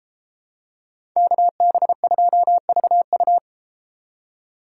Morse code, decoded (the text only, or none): KB2VU